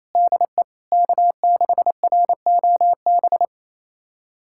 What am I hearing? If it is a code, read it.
Morse code, heard as a keyed tone, about 28 wpm, 700 Hz. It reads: DE K6ROB
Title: USA Callsign DE Callsign - 5 Characters - 28wpm